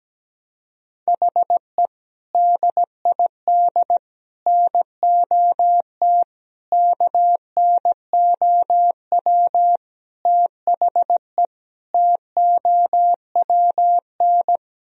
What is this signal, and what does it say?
Morse code, heard as a keyed tone, about 17 wpm, 700 Hz. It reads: HE DID NOT KNOW THE TOWN
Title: Sentences from Top 1000 Words - Review Part 2 - 17wpm